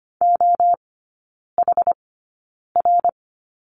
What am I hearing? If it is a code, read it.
Morse code, heard as a keyed tone, about 25 wpm, spaced as for 9 wpm, 700 Hz. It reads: OHR
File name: Lesson 17e - Letter u - Three Characters 25wpm